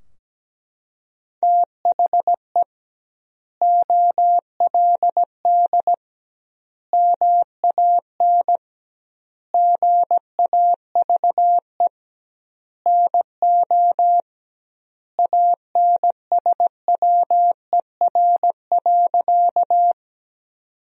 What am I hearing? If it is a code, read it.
Morse code, heard as a keyed tone, about 17 wpm, 700 Hz. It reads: THE OLD MAN GAVE NO ANSWER.
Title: 2x Word-Spacing - Sentences from Top 500 Words - 17wpm